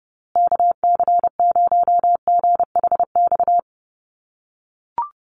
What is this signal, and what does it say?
Morse code, heard as a keyed tone, about 30 wpm, 700 Hz. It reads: KC0GHX
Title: USA Call Signs Repeated 3x 30wpm